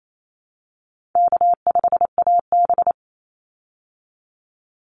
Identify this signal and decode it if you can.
Morse code, heard as a keyed tone, about 28 wpm, 700 Hz. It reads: K5AB
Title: Call Signs Repeated 3x - 28wpm